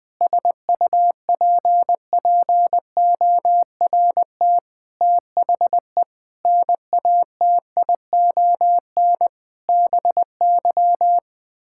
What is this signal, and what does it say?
Morse code, heard as a keyed tone, about 20 wpm, 700 Hz. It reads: SUPPORT THE NATION BY
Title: Sets of 9 Words - 20wpm